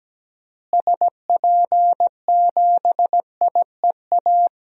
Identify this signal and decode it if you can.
Morse code, heard as a keyed tone, about 17 wpm, 700 Hz. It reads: SP7IEA
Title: DXCC-100 Call Signs - 6 Characters - Repeated 3x - 17wpm